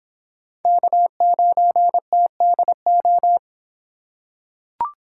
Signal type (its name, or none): Morse code